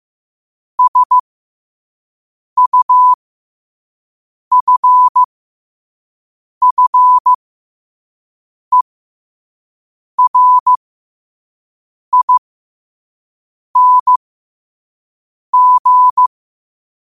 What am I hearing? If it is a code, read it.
Morse code, heard as a keyed tone, about 15 wpm, spaced as for 5 wpm, 1000 Hz. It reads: SUFFERING
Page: New quotes every day in morse code at 5 Words per minute.